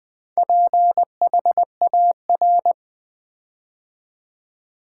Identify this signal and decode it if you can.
Morse code, heard as a keyed tone, about 20 wpm, 700 Hz. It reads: PHAR
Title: Build A Word - 4 Letters At A Time - 8-Letter Words - 20wpm